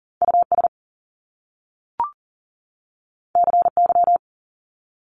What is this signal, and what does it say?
Morse code, heard as a keyed tone, about 41 wpm, 700 Hz. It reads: US CY
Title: Top Two Letter Combinations in Words 40wpm